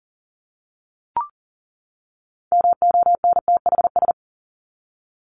Morse code, heard as a keyed tone, about 40 wpm, 700 Hz. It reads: MONTHS
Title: ES:S Suffix - Top 100 Words with Base Words 40wpm